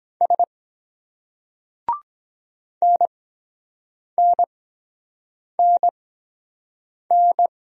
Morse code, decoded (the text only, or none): S NNNN